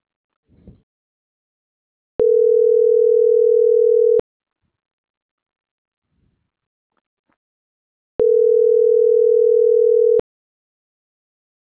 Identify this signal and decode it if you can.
Ringback tone (North America), 440 + 480 Hz.